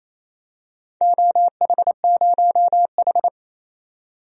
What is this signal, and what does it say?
Morse code, heard as a keyed tone, about 27 wpm, 690 Hz. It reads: OH0H
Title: DXCC-100 Call Signs - 4 Characters - 28wpm